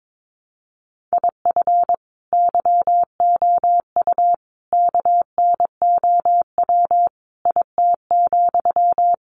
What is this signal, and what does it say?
Morse code, heard as a keyed tone, about 22 wpm, 700 Hz. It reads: IF YOU KNOW IT,
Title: Sentences from Top 400 Words 22wpm